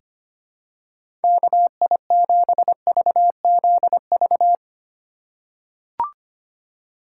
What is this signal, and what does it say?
Morse code, heard as a keyed tone, about 25 wpm, 700 Hz. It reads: KI7VZV
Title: USA Call Signs - 6 Characters - 8x Character Spacing - 25wpm